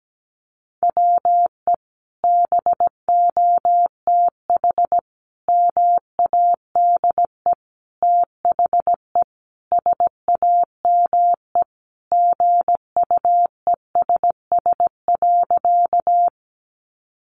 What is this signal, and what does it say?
Morse code, heard as a keyed tone, about 17 wpm, 700 Hz. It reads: WE BOTH MADE THE SAME GUESS.
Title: Sentences from Top 1500 Words 17wpm